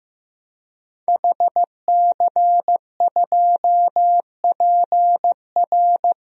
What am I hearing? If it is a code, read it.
Morse code, heard as a keyed tone, about 15 wpm, 700 Hz. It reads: HC2PR
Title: DXCC-100 Call Signs - 5 Characters - 15wpm